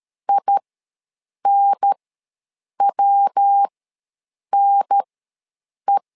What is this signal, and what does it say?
Morse code, heard as a keyed tone, about 13 wpm, spaced as for 7 wpm, 800 Hz. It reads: INWNE
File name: LEZIONI / 3 MWRO / 3 MW ws 12wpm 1816